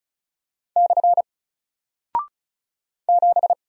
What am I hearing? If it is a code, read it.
Morse code, heard as a keyed tone, about 36 wpm, 700 Hz. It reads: / 7
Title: Single Character (Letter:Number:Punctuation:Prosign) Alt 35wpm